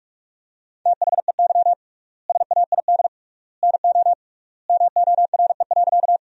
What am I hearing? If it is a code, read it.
Morse code, heard as a keyed tone, about 46 wpm, 700 Hz. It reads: THEY SAID NO MORE.
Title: 3x Word-Spacing - Sentences from Top 100 Words - 45wpm